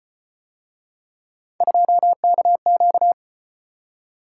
Morse code, heard as a keyed tone, about 31 wpm, 700 Hz. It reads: 2KQ
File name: Speed-Racing - Three Letter-Number Combinations - 25wpm